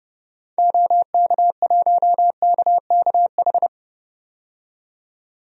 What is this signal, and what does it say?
Morse code, heard as a keyed tone, about 32 wpm, 700 Hz. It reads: OK1KKH
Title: DXCC-100 Call Signs - 6 Characters - 30wpm